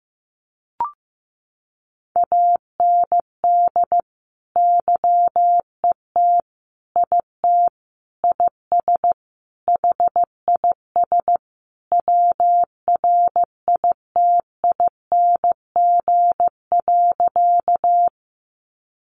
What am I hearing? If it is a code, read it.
Morse code, heard as a keyed tone, about 15 wpm, 700 Hz. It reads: AND YET IT IS HIS WRITING.
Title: Sentences from Top 800 Words 15wpm